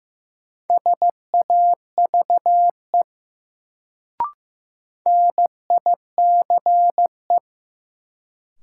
Morse code, heard as a keyed tone, about 15 wpm, 700 Hz. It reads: SAVE NICE
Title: Sentences from Top 1100 Words - New Word Drill Alt - 15wpm